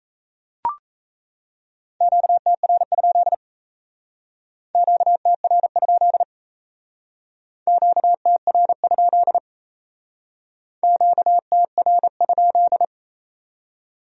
Morse code, heard as a keyed tone, about 36 wpm, 700 Hz. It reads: QTR? QTR? QTR? QTR?